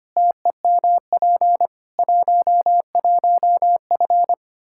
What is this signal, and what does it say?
Morse code, heard as a keyed tone, about 25 wpm, 700 Hz. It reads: TEMP 11F
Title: Practice QSO - Exchange 2 - Info - 25wpm